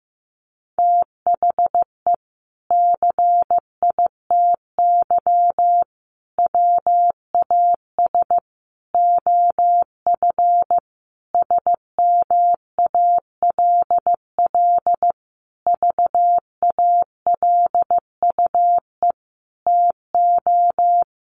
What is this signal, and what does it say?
Morse code, heard as a keyed tone, about 15 wpm, 700 Hz. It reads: THE CITY WAS OF SMALL VALUE TO
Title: Sentences from Top 1500 Words 15wpm